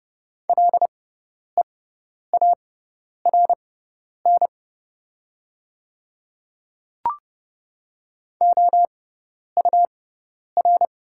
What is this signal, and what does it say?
Morse code, heard as a keyed tone, about 30 wpm, spaced as for 10 wpm, 690 Hz. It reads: LEARN OUR
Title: 6x Character Spacing - Top 500 Words - 30wpm